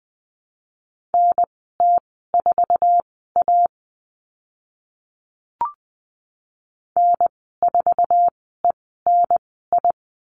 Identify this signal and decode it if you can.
Morse code, heard as a keyed tone, about 20 wpm, spaced as for 15 wpm, 700 Hz. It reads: NT4A N4ENI
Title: 062 Lesson 11g - Letter h - Call Signs 20wpm